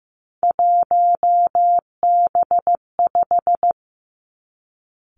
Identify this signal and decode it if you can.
Morse code, heard as a keyed tone, about 15 wpm, 700 Hz. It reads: 1B5
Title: Three Challenging Letter-Number Combinations 15wpm